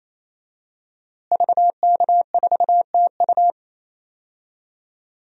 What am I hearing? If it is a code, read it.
Morse code, heard as a keyed tone, about 28 wpm, 700 Hz. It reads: VK4TU
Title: DXCC-100 Call Signs - 5 Characters - Repeated 3x - 28wpm